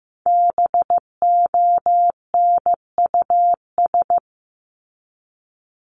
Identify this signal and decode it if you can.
Morse code, heard as a keyed tone, about 15 wpm, 700 Hz. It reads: BONUS
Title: Top 1751-2000 Words 15wpm